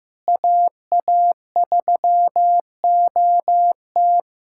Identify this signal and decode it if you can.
Morse code, heard as a keyed tone, about 15 wpm, 700 Hz. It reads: AA3OT
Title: USA Callsign DE Callsign - 5 Characters - 15wpm